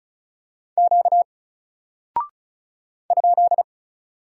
Morse code, heard as a keyed tone, about 36 wpm, 700 Hz. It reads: Q ?
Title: Single Character (Letter:Number:Punctuation:Prosign) Alt 35wpm